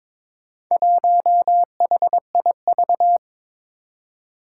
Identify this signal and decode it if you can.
Morse code, heard as a keyed tone, about 22 wpm, 700 Hz. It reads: 1HIV
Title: Four Challenging Letter-Number Combinations 22wpm